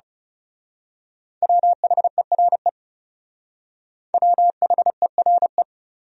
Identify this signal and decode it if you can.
Morse code, heard as a keyed tone, about 34 wpm, 700 Hz. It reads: WHERE WHERE
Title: Speed-Racing Top 200 Words 30wpm